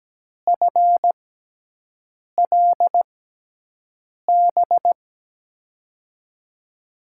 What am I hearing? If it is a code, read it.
Morse code, heard as a keyed tone, about 17 wpm, spaced as for 6 wpm, 700 Hz. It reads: FLB
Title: USA Call Signs - 6 Characters - 6x Character Spacing - 17wpm